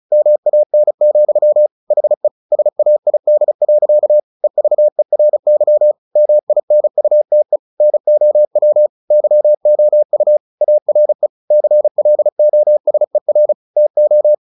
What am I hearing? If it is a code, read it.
Morse code, heard as a keyed tone, about 34 wpm, 600 Hz. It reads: MAN, HE SAID. EVERY MINUTE NOW YOU ARE CLOSER TO
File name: the old man and the sea 2 35wpm